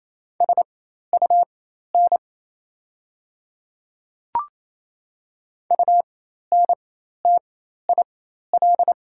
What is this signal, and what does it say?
Morse code, heard as a keyed tone, about 28 wpm, spaced as for 13 wpm, 690 Hz. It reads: SUN UNTIL
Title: 4x Character Spacing - Top 300 Words - 28wpm